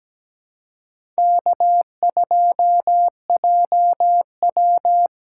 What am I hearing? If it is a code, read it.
Morse code, heard as a keyed tone, about 17 wpm, 700 Hz. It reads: K2JW
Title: USA Callsign DE Callsign - 4 Characters - 17wpm